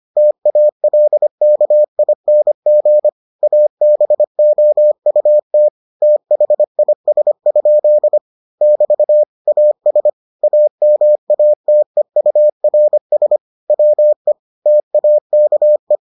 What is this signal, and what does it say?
Morse code, heard as a keyed tone, about 25 wpm, 600 Hz. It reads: TALKING ABOUT THIS? <BT> AS AMATEURS WE TAKE